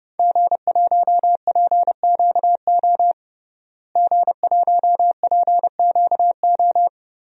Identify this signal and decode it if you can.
Morse code, heard as a keyed tone, about 30 wpm, 700 Hz. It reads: G1PQO G1PQO
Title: DXCC-100 Call Signs - 5 Characters - Repeated 3x - 30wpm